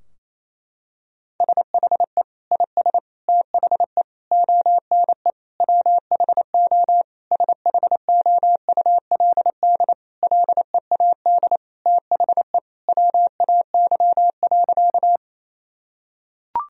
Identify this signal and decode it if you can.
Morse code, heard as a keyed tone, about 27 wpm, 700 Hz. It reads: SHE IS THE ONE WHO SHOULD LEAD THE WAY.
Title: Sentences from Top 1000 Words - Review Part 2 - 28wpm